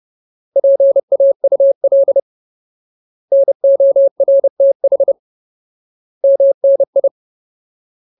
Morse code, heard as a keyed tone, about 30 wpm, 550 Hz. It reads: PAUL NORTH MNI
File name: Ultimate Word List / 30WPM Ultimate